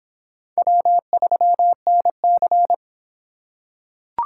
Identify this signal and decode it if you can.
Morse code, heard as a keyed tone, about 23 wpm, 700 Hz. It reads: W3NC